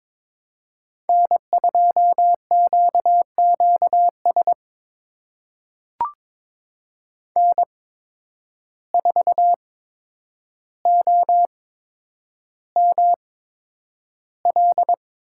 Morse code, heard as a keyed tone, about 22 wpm, spaced as for 9 wpm, 700 Hz. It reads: N2QQS N4OML
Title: USA Call Signs - 5 Characters - 8x Character Spacing - 22wpm